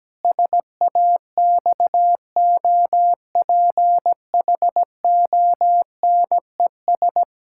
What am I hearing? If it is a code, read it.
Morse code, heard as a keyed tone, about 17 wpm, 700 Hz. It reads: SAXOPHONES
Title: Pangram Challenge v2 - 17wpm